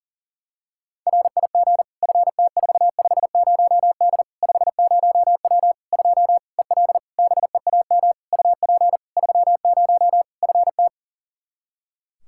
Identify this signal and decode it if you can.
Morse code, heard as a keyed tone, about 40 wpm, 700 Hz. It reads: RIG FT450D 50W 2 EL BEAM UP 30 FT
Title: Practice QSO - Exchange 2 - Info - 40wpm